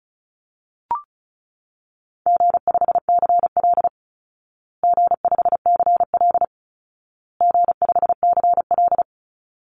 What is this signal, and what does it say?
Morse code, heard as a keyed tone, about 34 wpm, 700 Hz. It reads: G5CL G5CL G5CL
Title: Call Signs Repeated 2x - 35wpm